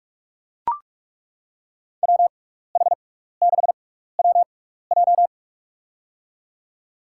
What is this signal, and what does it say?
Morse code, heard as a keyed tone, about 42 wpm, spaced as for 15 wpm, 700 Hz. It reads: WH6WJ
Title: USA Call Signs - 5 Characters - 6x Character Spacing - 45wpm